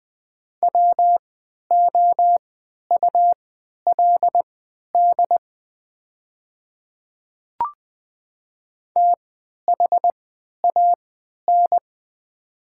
Morse code, heard as a keyed tone, about 20 wpm, spaced as for 11 wpm, 700 Hz. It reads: WOULD THAN
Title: 3x Character Spacing - Top 100 Words - 20wpm